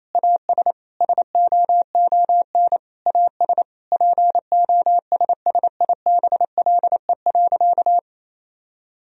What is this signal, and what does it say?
Morse code, heard as a keyed tone, about 28 wpm, 700 Hz. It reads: AS SOON AS POSSIBLE.
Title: Sentences from Top 1000 Words - Review Part 2 - 28wpm